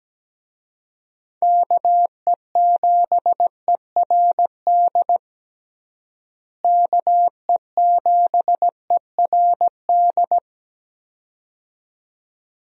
Morse code, heard as a keyed tone, about 17 wpm, 700 Hz. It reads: KE7ERD KE7ERD
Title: USA Call Signs - 6 Characters - Repeated 2x - 17wpm